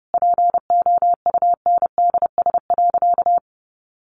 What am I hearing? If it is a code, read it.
Morse code, heard as a keyed tone, about 30 wpm, 700 Hz. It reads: POUNDS.
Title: Easy Phrases 30wpm